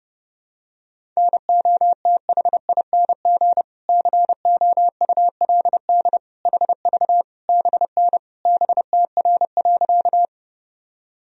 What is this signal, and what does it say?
Morse code, heard as a keyed tone, about 31 wpm, 700 Hz. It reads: NOTHING COULD HV BN BTR.
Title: Sentences from Top 500 Words and Abbreviations 30wpm